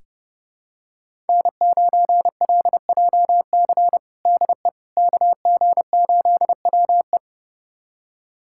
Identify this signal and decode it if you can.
Morse code, heard as a keyed tone, about 30 wpm, 700 Hz. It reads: N9LJC DE KG8WE
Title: USA Callsign DE Callsign - 5 Characters - 30wpm